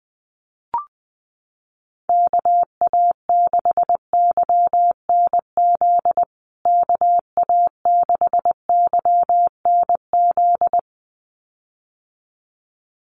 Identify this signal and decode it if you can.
Morse code, heard as a keyed tone, about 20 wpm, 700 Hz. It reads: KA6YNZ KA6YNZ
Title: USA Call Signs Repeated 2x 20wpm